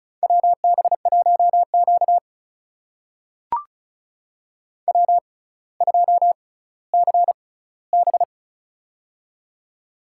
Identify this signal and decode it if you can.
Morse code, heard as a keyed tone, about 36 wpm, spaced as for 18 wpm, 700 Hz. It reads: WB1Q W2CB
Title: USA Call Signs - 4 Characters - 6x Character Spacing - 35wpm